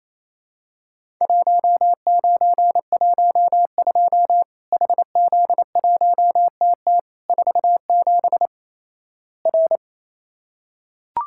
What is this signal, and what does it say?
Morse code, heard as a keyed tone, about 28 wpm, 700 Hz. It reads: 1912 HZ1TT 47 R
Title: Worked All Europe DX Contest - QTC Exchange - 28wpm